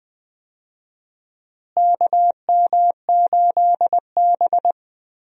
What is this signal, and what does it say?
Morse code, heard as a keyed tone, about 20 wpm, 700 Hz. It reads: KM8B
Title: USA Call Signs - 4 Characters - Repeated 2x - 20wpm